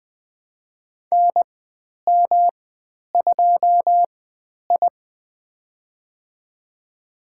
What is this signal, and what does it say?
Morse code, heard as a keyed tone, about 20 wpm, spaced as for 10 wpm, 700 Hz. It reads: NM2I
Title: 109 Lesson 18g - Letter m - Call Signs 20wpm